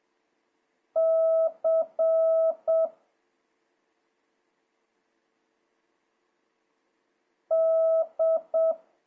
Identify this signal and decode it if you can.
Morse code, heard as a keyed tone, about 7 wpm, 650 Hz. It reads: C D
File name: jh8cbh / 2026cw